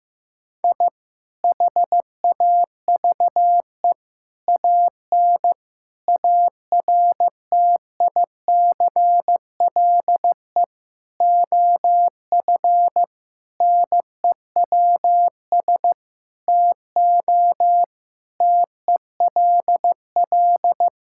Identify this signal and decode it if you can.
Morse code, heard as a keyed tone, about 16 wpm, 700 Hz. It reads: I HAVE AN ARTICLE OF NEWS TO TELL
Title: Sentences from Top 1000 Words - Review Part 5 - 15wpm